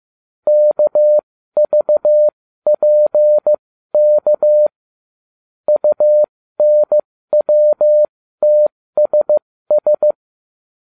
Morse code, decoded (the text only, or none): KVPK UNWTSS